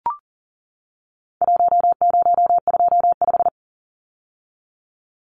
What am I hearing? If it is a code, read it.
Morse code, heard as a keyed tone, about 40 wpm, 700 Hz. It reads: 1025
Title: Numbers - Contest Style - 4-Digit 40wpm